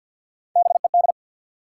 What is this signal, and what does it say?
Morse code, heard as a keyed tone, about 53 wpm, 700 Hz. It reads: BED
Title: Three Letter Words - 50wpm